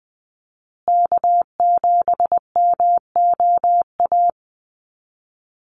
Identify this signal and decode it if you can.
Morse code, heard as a keyed tone, about 20 wpm, 700 Hz. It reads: K7MOA
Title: Call Signs Repeated 2x - 20wpm